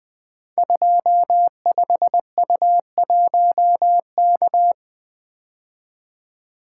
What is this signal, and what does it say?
Morse code, heard as a keyed tone, about 21 wpm, 700 Hz. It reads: 25U1K
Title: Code Groups - 5 Letter:Numbers - 20wpm